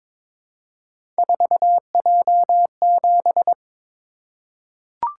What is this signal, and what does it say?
Morse code, heard as a keyed tone, about 22 wpm, 700 Hz. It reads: 4J7
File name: practice / 22wpm / Three Letter-Number Combinations - 22wpm